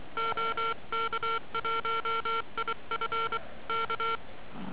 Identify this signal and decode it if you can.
Morse code, heard as a keyed tone, about 23 wpm, 1360 Hz, with noise. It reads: OK1IF K